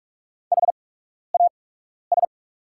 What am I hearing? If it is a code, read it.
Morse code, heard as a keyed tone, about 44 wpm, spaced as for 12 wpm, 690 Hz. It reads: HAS
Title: 8x Character Spacing - Top 100 Words - 45wpm